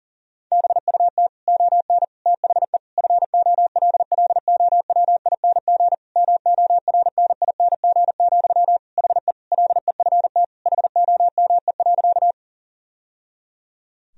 Morse code, heard as a keyed tone, about 40 wpm, 700 Hz. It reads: BUT ON THE FOLLOWING MORNING, HE LEFT HOME.
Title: Sentences from Top 1000 Words - Review Part 4 - 40wpm